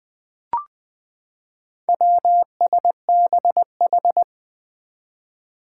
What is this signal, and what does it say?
Morse code, heard as a keyed tone, about 20 wpm, 700 Hz. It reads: WSBH
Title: Four Challenging Letter-Number Combinations 20wpm